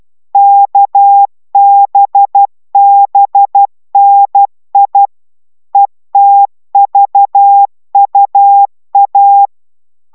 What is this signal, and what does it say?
Morse code, heard as a keyed tone, about 12 wpm, 800 Hz. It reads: KBBNI ETVUA